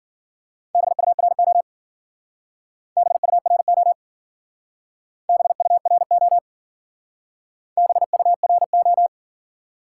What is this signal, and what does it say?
Morse code, heard as a keyed tone, about 50 wpm, 700 Hz. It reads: BURO BURO BURO BURO